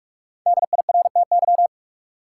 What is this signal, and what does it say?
Morse code, heard as a keyed tone, about 42 wpm, 700 Hz. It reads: DIRTY
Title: Five Letter Words - 45wpm